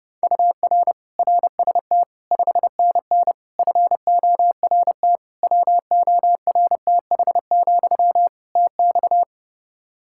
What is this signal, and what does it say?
Morse code, heard as a keyed tone, about 32 wpm, 700 Hz. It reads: UR RST 5NN FORT WORTH, TX
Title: Practice QSO - Exchange 1 - Beginning - 30wpm